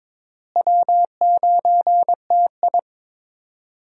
Morse code, heard as a keyed tone, about 21 wpm, 700 Hz. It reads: W9TI
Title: USA Call Signs - 4 Characters - Repeated 3x - 22wpm